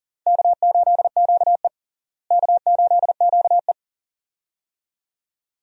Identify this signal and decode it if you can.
Morse code, heard as a keyed tone, about 40 wpm, 700 Hz. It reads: K8QE K8QE
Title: USA Call Signs - 4 Characters - Repeated 2x - 40wpm